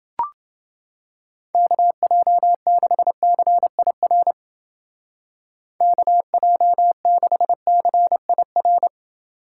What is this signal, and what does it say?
Morse code, heard as a keyed tone, about 28 wpm, 700 Hz. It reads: KJ6CIR KJ6CIR